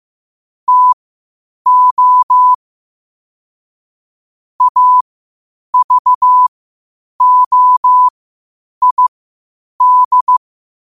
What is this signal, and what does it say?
Morse code, heard as a keyed tone, about 15 wpm, spaced as for 8 wpm, 1000 Hz. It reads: TO AVOID
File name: qotdfiles / 8wpm